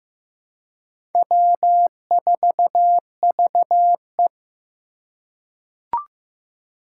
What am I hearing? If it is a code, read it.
Morse code, heard as a keyed tone, about 15 wpm, 700 Hz. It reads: W4VE
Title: Four Challenging Letter-Number Combinations 15wpm